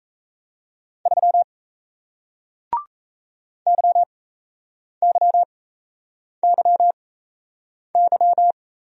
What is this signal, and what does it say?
Morse code, heard as a keyed tone, about 37 wpm, spaced as for 8 wpm, 700 Hz. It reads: 3 YYYY